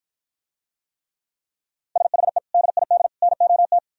Morse code, heard as a keyed tone, about 52 wpm, 700 Hz. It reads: SHE DID NOT